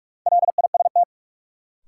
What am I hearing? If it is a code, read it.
Morse code, heard as a keyed tone, about 46 wpm, 700 Hz. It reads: LIST
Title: Sentences from Top 400 Words - New Word Drill - 45wpm